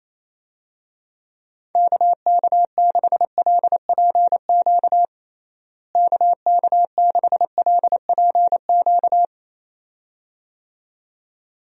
Morse code, heard as a keyed tone, about 28 wpm, 700 Hz. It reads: KK6LPQ KK6LPQ
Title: USA Call Signs - 6 Characters - Repeated 2x - 28wpm